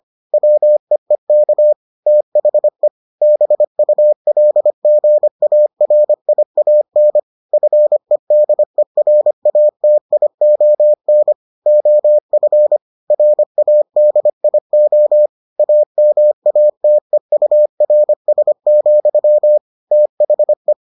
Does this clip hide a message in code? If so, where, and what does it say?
Yes: Morse code from 0.3 to 20.7 s, heard as a keyed tone, about 25 wpm, 600 Hz. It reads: WEEK THE BULGARIAN FEDERATION OF RADIO AMATEURS, THE